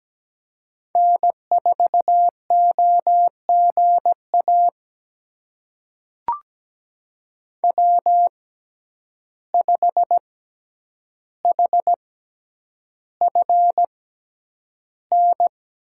Morse code, heard as a keyed tone, about 17 wpm, spaced as for 9 wpm, 700 Hz. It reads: N4OGA W5HFN
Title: USA Call Signs - 5 Characters - 6x Character Spacing - 17wpm